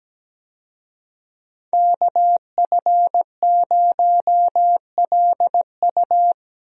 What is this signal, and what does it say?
Morse code, heard as a keyed tone, about 17 wpm, 700 Hz. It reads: KF0LU
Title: USA Call Signs - 5 Characters - Repeated 3x - 17wpm